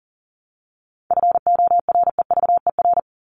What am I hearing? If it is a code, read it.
Morse code, heard as a keyed tone, about 40 wpm, 700 Hz. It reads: FOREVER
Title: Top 1501-1750 Words 40wpm